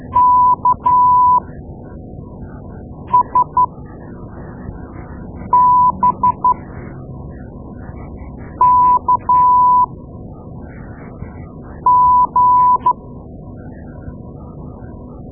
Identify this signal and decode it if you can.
Morse code, heard as a keyed tone, about 12 wpm, spaced as for 4 wpm, 990 Hz. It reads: KSBKG
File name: Morse Lessons / Morse Lessons / Tape-2A